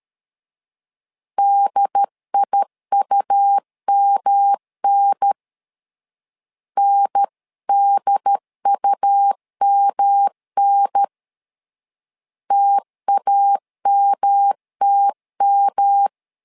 Morse code, heard as a keyed tone, about 13 wpm, 800 Hz. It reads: DIUMN NDUMN TAMTM
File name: LEZIONI / 3 MWRO / 3 M 12wpm 1541